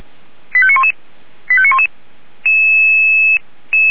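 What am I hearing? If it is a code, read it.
Five-tone selective call (ZVEI), heard as quick runs of five tones, tones of about 70 ms. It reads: 85911 85911
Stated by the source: ZVEI five-tone fire brigade alert received on an SDR, Orlamünde.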